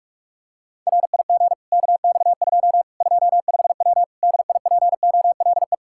Morse code, heard as a keyed tone, about 46 wpm, 700 Hz. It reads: RIG KX2 25W DIPOLE
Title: Practice QSO - Exchange 2 - Info - 45wpm